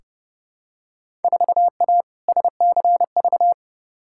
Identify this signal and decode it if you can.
Morse code, heard as a keyed tone, about 29 wpm, 700 Hz. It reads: 4A SCV
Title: Field Day Exchanges Encore - 30wpm